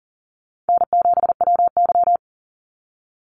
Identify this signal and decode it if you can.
Morse code, heard as a keyed tone, about 40 wpm, 700 Hz. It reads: N7WY
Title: Call Signs Repeated 3x - 40wpm